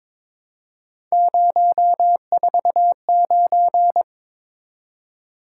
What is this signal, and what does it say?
Morse code, heard as a keyed tone, about 28 wpm, 700 Hz. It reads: 049
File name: Speed-Racing - Three Letter-Number Combinations - 22wpm